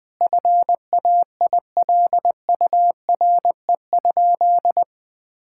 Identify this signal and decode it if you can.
Morse code, heard as a keyed tone, about 20 wpm, 700 Hz. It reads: FAILURE?
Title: Sentences from Top 1000 Words - Review Part 6 - 20wpm